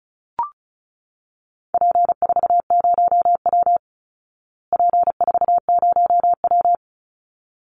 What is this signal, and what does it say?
Morse code, heard as a keyed tone, about 34 wpm, 700 Hz. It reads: P40W P40W
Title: Call Signs Repeated 3x - 35wpm